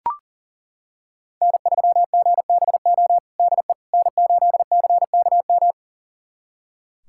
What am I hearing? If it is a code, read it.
Morse code, heard as a keyed tone, about 37 wpm, 700 Hz. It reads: N3GBO DE N8CKM
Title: USA Callsign DE Callsign - 5 Characters - 40wpm